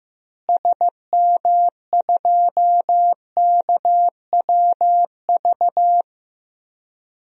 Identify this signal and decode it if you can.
Morse code, heard as a keyed tone, about 15 wpm, 700 Hz. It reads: SM2KWV
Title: DXCC-100 Call Signs - 6 Characters - Repeated 2x - 15wpm